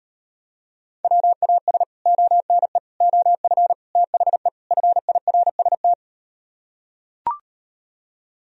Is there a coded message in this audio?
Yes: Morse code from 1.0 to 5.9 s, heard as a keyed tone, about 38 wpm, 700 Hz. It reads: WAS ONE OF THE FIRST